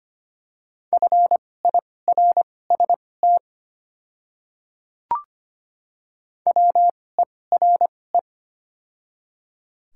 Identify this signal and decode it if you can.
Morse code, heard as a keyed tone, about 25 wpm, spaced as for 18 wpm, 700 Hz. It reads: FIRST WERE
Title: 2x Character Spacing - Top 100 Words - 25wpm